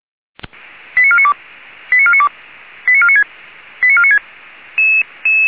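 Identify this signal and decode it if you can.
Five-tone selective call (ZVEI), heard as quick runs of five tones, tones of about 70 ms. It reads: 89492 89492 89496 89496
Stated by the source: ZVEI five-tone selective call received by SDR.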